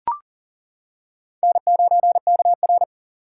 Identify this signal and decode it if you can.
Morse code, heard as a keyed tone, about 38 wpm, 700 Hz. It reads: N9KR
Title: K1USN SST QSO Practice - 40wpm